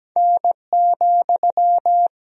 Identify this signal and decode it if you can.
Morse code, heard as a keyed tone, about 17 wpm, 700 Hz. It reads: N,
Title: Sentences from Top 300 Words and Abbreviations 17wpm